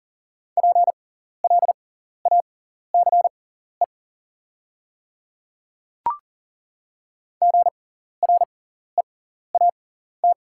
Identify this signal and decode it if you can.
Morse code, heard as a keyed tone, about 39 wpm, spaced as for 14 wpm, 690 Hz. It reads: PLACE GREAT
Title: 6x Character Spacing - Top 500 Words - 40wpm